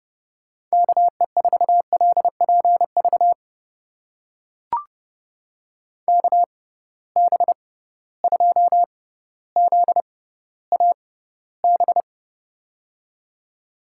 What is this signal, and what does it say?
Morse code, heard as a keyed tone, about 30 wpm, spaced as for 15 wpm, 700 Hz. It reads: KE4LPV KB2ZAB
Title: USA Call Signs - 6 Characters - 6x Character Spacing - 30wpm